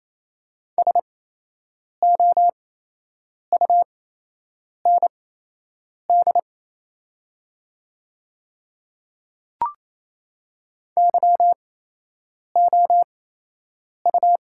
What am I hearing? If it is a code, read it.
Morse code, heard as a keyed tone, about 28 wpm, spaced as for 8 wpm, 700 Hz. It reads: SOUND YOU
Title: 8x Character Spacing - Top 100 Words - 28wpm